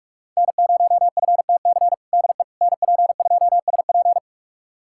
Morse code, heard as a keyed tone, about 45 wpm, 700 Hz. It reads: N0FTC DE NP2SP
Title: USA Callsign DE Callsign - 5 Characters - 45wpm